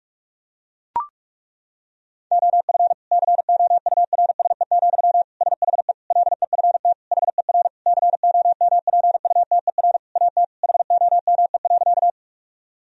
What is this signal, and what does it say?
Morse code, heard as a keyed tone, about 41 wpm, 700 Hz. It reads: OF COURSE, SHE LEFT HER COMPUTER AT HOME.
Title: Sentences from Top 1000 Words - Review Part 3 - 45wpm